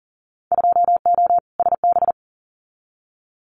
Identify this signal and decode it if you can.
Morse code, heard as a keyed tone, about 40 wpm, 700 Hz. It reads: 2O SB
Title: Winter Field Day Exchanges 40wpm